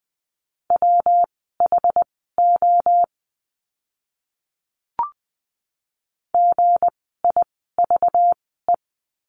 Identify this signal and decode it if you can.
Morse code, heard as a keyed tone, about 20 wpm, spaced as for 14 wpm, 700 Hz. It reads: WHO GIVE
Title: 237 Lesson 38f - Letter q - Words 20wpm